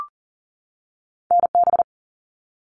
Morse code, heard as a keyed tone, about 38 wpm, 700 Hz. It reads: NB
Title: Canadian Provinces and Territories 40wpm